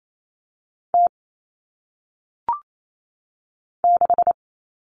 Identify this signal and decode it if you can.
Morse code, heard as a keyed tone, about 28 wpm, 700 Hz. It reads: T 6
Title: All Cut Numbers - 1-Digit 28wpm